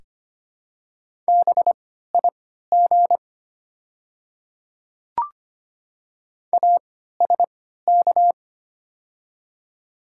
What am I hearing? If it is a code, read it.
Morse code, heard as a keyed tone, about 25 wpm, spaced as for 14 wpm, 700 Hz. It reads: BIG ASK
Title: 3x Character Spacing - Top 200 Words - 25wpm